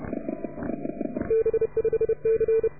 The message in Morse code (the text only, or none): B5C